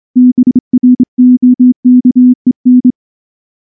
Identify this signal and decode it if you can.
Morse code, heard as a keyed tone, about 23 wpm, 250 Hz. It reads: BROKEN
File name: morse / morse-training-25WPM